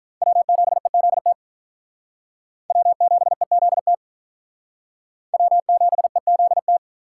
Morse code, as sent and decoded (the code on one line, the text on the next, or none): .-- --... --.. - / .-- --... --.. - / .-- --... --.. -
W7EZT W7EZT W7EZT